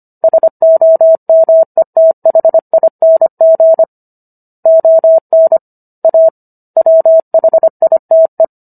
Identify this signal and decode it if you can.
Morse code, heard as a keyed tone, about 25 wpm, 650 Hz. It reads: SOMETHING ON A WHITE